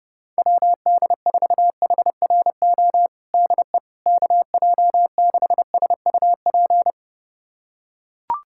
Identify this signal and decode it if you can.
Morse code, heard as a keyed tone, about 30 wpm, 700 Hz. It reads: WD4HRO DE KJ6SUP
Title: USA Callsign DE Callsign 30wpm